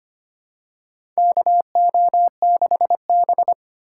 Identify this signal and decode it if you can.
Morse code, heard as a keyed tone, about 25 wpm, 700 Hz. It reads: KO6B
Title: USA Call Signs - 3x Character Spacing - 25wpm